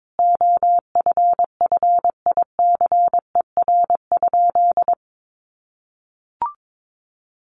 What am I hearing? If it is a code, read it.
Morse code, heard as a keyed tone, about 22 wpm, 700 Hz. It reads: OFFICER?
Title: Sentences from Top 1200 Words 22wpm